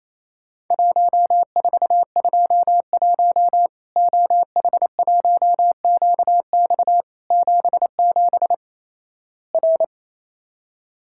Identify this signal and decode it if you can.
Morse code, heard as a keyed tone, about 28 wpm, 700 Hz. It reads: 1421 OH1QX 77 R
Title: Worked All Europe DX Contest - QTC Exchange - 28wpm